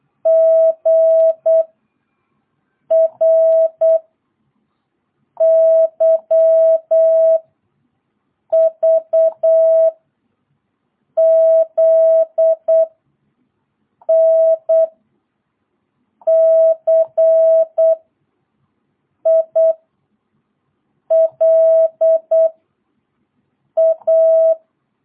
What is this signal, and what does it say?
Morse code, heard as a keyed tone, about 8 wpm, 660 Hz. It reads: G R Y V Z N C I L A
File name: jh8cbh / 2022cw